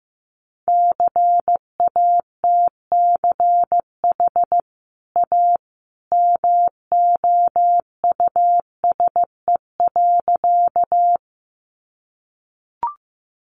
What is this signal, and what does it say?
Morse code, heard as a keyed tone, about 15 wpm, 700 Hz. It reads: CATCH A MOUSE.
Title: Sentences from Top 1500 Words 15wpm